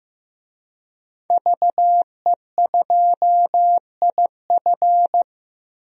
Morse code, heard as a keyed tone, about 15 wpm, 700 Hz. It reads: VE2IF
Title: DXCC-100 Call Signs - 5 Characters - Repeated 3x - 15wpm